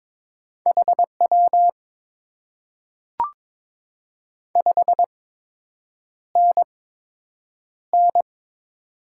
Morse code, heard as a keyed tone, about 23 wpm, spaced as for 8 wpm, 700 Hz. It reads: HW 5NN
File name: Top 90 QSO Elements - 8x Character Spacing - 22wpm